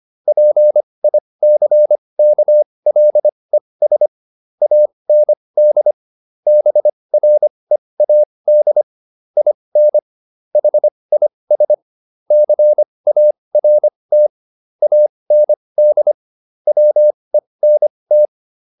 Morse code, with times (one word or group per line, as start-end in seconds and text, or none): 0.3-4.1 PICKLES
4.6-5.9 AND
6.5-8.8 BREAD
9.4-10.0 IN
10.6-11.7 HIS
12.3-14.3 CART
14.8-16.1 AND
16.7-18.3 WENT